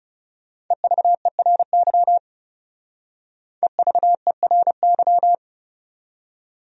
Morse code, heard as a keyed tone, about 37 wpm, 700 Hz. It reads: EVERY EVERY
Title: Speed-Racing Top 200 Words 30wpm